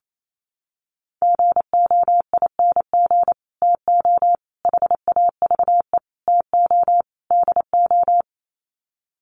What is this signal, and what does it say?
Morse code, heard as a keyed tone, about 28 wpm, 700 Hz. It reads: GOING TO HAVE TO DO
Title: Sets of 5 Words Encore 28wpm